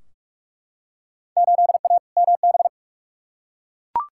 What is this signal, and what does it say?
Morse code, heard as a keyed tone, about 47 wpm, 700 Hz. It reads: 8A MB
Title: Field Day Exchanges Encore - 45wpm